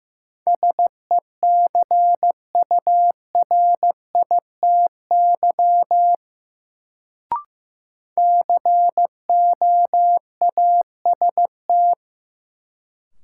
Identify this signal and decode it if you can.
Morse code, heard as a keyed tone, about 15 wpm, 700 Hz. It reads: SECURITY COAST
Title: Sentences from Top 1000 Words - Review Part 4 - New Word Drill - 15wpm